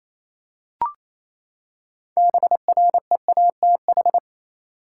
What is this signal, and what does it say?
Morse code, heard as a keyed tone, about 28 wpm, 700 Hz. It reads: BREATH
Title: Six Letter Words - 28wpm